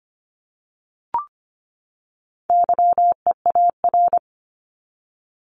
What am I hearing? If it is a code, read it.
Morse code, heard as a keyed tone, about 25 wpm, 700 Hz. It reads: YEAR
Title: ES:S Suffix - Top 100 Words with Base Words 25wpm